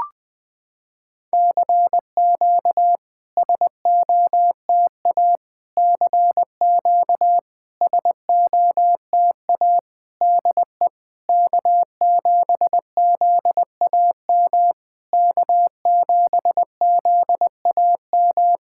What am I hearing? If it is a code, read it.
Morse code, heard as a keyed tone, about 20 wpm, 700 Hz. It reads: CQ SOTA CQ SOTA DE K7ZAM K7ZAM
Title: SOTA QSO Practice 20wpm